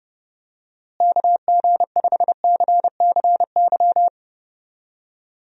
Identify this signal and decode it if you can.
Morse code, heard as a keyed tone, about 32 wpm, 700 Hz. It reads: KG5CCY
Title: USA Call Signs - 6 Characters - Repeated 2x - 30wpm